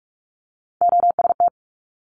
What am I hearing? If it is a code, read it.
Morse code, heard as a keyed tone, about 46 wpm, 700 Hz. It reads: OST
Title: Multi-letter Phonemes Quick Fire 45wpm